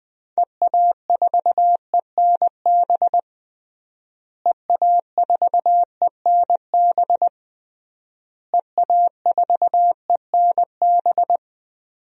Morse code, heard as a keyed tone, about 21 wpm, 700 Hz. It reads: EA4ENB EA4ENB EA4ENB
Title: DXCC-100 Call Signs - 6 Characters - Repeated 3x - 20wpm